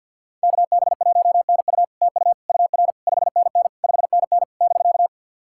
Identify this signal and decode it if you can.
Morse code, heard as a keyed tone, about 50 wpm, 700 Hz. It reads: KB1NU TU UR 5NN 5NN <BK>
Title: SOTA QSO Practice 50wpm